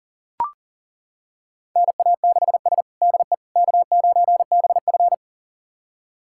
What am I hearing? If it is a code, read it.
Morse code, heard as a keyed tone, about 40 wpm, 700 Hz. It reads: NA6S DE K9BF
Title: USA Callsign DE Callsign - 4 Characters - 40wpm